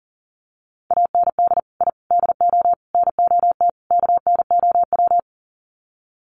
Morse code, heard as a keyed tone, about 40 wpm, 700 Hz. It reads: AND I DO NOT KNOW
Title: Sets of 5 Words 40wpm